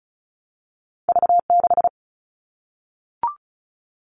Morse code, heard as a keyed tone, about 36 wpm, 700 Hz. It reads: V6
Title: All Cut Numbers - 2-Digit 35wpm